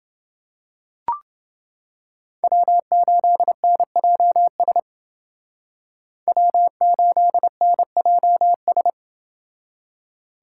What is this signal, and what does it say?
Morse code, heard as a keyed tone, about 29 wpm, 700 Hz. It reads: W8NJS W8NJS